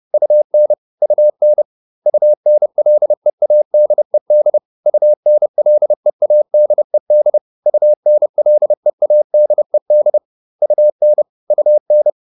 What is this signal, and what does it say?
Morse code, heard as a keyed tone, about 30 wpm, 600 Hz. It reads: UN UN UNLEADED UNLEADED UNLEADED UN UN